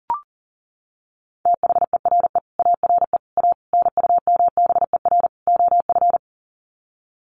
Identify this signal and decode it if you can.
Morse code, heard as a keyed tone, about 40 wpm, 700 Hz. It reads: THERE ARE A NUMBER OF
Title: Sets of 5 Words 40wpm